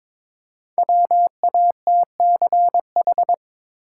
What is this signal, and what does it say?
Morse code, heard as a keyed tone, about 23 wpm, 700 Hz. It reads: WATCH
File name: 5 Letter Words - First 3 Letters Given - 22wpm